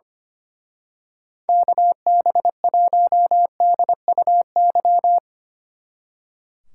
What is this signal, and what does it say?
Morse code, heard as a keyed tone, about 24 wpm, 700 Hz. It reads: KB1DUY
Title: Speed Racing - USA 2x3 Call Signs 25wpm